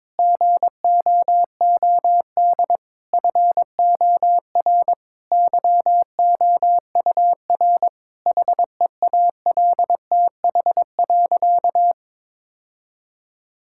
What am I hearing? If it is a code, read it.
Morse code, heard as a keyed tone, about 22 wpm, 700 Hz. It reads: GOOD FOR YOUR HEALTH.
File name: Difficult Sentences with Hints - 22wpm